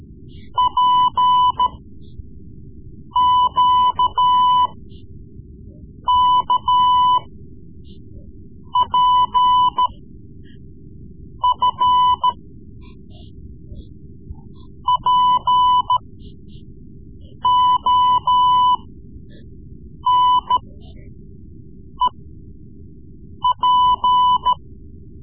Morse code, heard as a keyed tone, about 14 wpm, spaced as for 5 wpm, 1000 Hz. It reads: PQKPF PONEP
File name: Morse Lessons / Morse Lessons / Tape-3B